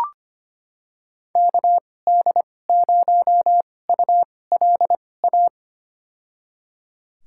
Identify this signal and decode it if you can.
Morse code, heard as a keyed tone, about 25 wpm, spaced as for 18 wpm, 700 Hz. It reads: KD0ULA
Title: USA Call Signs - 6 Characters - 2x Character Spacing - 25wpm